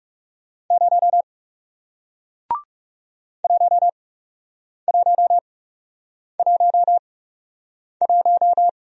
Morse code, heard as a keyed tone, about 39 wpm, spaced as for 8 wpm, 700 Hz. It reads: 0 1111